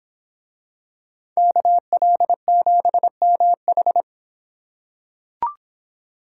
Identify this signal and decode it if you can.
Morse code, heard as a keyed tone, about 26 wpm, 700 Hz. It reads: KL7MH